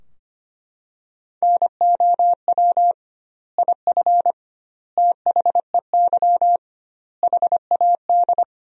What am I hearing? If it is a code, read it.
Morse code, heard as a keyed tone, about 25 wpm, 700 Hz. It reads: NOW IF THEY HAD
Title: 2x Word-Spacing - Sentences from Top 500 Words - 25wpm